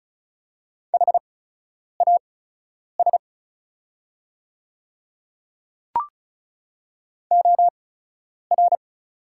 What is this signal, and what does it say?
Morse code, heard as a keyed tone, about 34 wpm, spaced as for 10 wpm, 690 Hz. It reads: HAS OR
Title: 8x Character Spacing - Top 100 Words - 35wpm